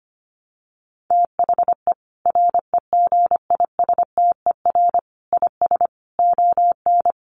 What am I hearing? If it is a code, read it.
Morse code, heard as a keyed tone, about 25 wpm, 700 Hz. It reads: THE REGISTER IS ON
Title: Sentences from Top 1200 Words 25wpm